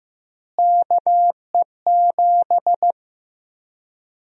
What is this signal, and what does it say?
Morse code, heard as a keyed tone, about 15 wpm, 700 Hz. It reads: KE7
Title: USA Call Sign Prefixes - Repeated 2x - 15wpm